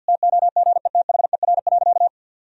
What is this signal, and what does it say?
Morse code, heard as a keyed tone, about 52 wpm, 700 Hz. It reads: TOGETHER.
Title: Sentences from Top 1000 Words - Review Part 1 - 50wpm